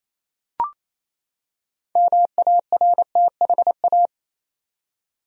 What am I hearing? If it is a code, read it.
Morse code, heard as a keyed tone, about 28 wpm, 700 Hz. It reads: MARTHA
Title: USA Common Names - 28wpm